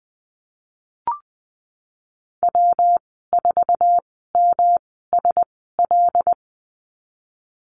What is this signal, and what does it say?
Morse code, heard as a keyed tone, about 20 wpm, spaced as for 15 wpm, 700 Hz. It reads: W4MSL
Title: 128 Lesson 21g - Number 6 - Call Signs 20wpm